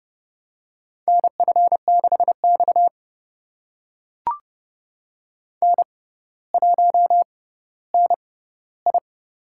Morse code, heard as a keyed tone, about 30 wpm, spaced as for 15 wpm, 700 Hz. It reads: NF6X N1NI
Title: USA Call Signs - 4 Characters - 6x Character Spacing - 30wpm